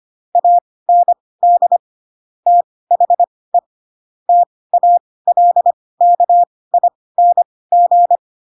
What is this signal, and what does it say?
Morse code, heard as a keyed tone, about 25 wpm, spaced as for 18 wpm, 700 Hz. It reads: AND THE TALKING
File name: Pinocchio / Pinocchio ch4 12 25 ik1hge